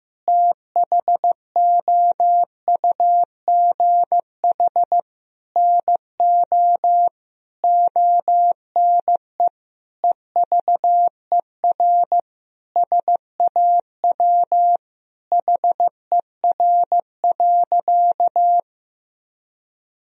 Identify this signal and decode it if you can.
Morse code, heard as a keyed tone, about 15 wpm, 700 Hz. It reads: THOUGH NO ONE EVER SAW HER.
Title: Sentences from Top 1000 Words - Review Part 2 - 15wpm